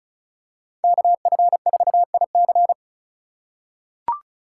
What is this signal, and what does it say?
Morse code, heard as a keyed tone, about 36 wpm, 700 Hz. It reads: KF4IC
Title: USA Call Signs - 4x Character Spacing - 35wpm